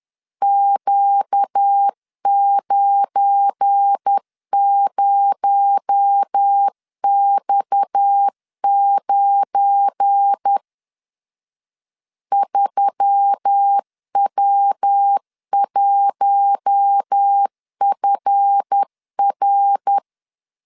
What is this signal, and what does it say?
Morse code, heard as a keyed tone, about 10 wpm, 800 Hz. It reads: Q90X9 3W1FR
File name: LEZIONI / 7 GF9Z / 7 GF9 10wpm 2747